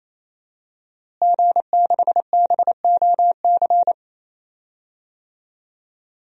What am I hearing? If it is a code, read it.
Morse code, heard as a keyed tone, about 28 wpm, 700 Hz. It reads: G6BOC
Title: DXCC-100 Call Signs - 5 Characters - Repeated 3x - 28wpm